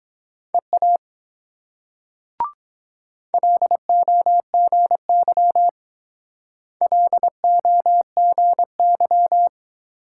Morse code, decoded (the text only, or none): EA LOGY LOGY